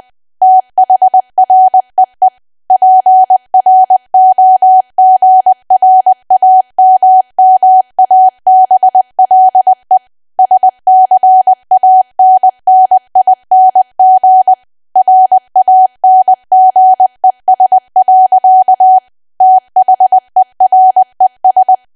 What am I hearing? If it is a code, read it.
Morse code, heard as a keyed tone, about 20 wpm, 750 Hz. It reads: THREE PROGRAMMABLE SCANNING RANGES. THERES
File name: Archive / 20 WPM / 20WPM